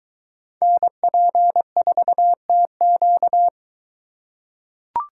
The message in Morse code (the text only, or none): NP4TQ